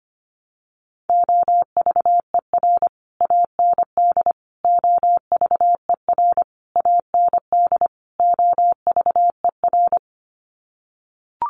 Morse code, heard as a keyed tone, about 25 wpm, 700 Hz. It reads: OVER AND OVER AND OVER
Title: Sets of 5 Words Encore 25wpm